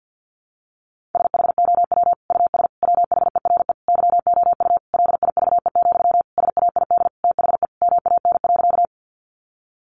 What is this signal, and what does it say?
Morse code, heard as a keyed tone, about 50 wpm, 700 Hz. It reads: SHOW US WHERE YOU LIVE, SAID THE MAN.
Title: Sentences from Top 200 Words 50wpm